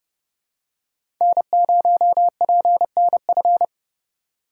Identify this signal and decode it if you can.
Morse code, heard as a keyed tone, about 30 wpm, 700 Hz. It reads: N0PNF
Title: USA Call Signs - 5 Characters - 4x Character Spacing - 30wpm